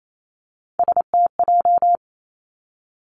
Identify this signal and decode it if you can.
Morse code, heard as a keyed tone, about 28 wpm, 700 Hz. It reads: STJ
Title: Three Challenging Letter-Number Combinations 28wpm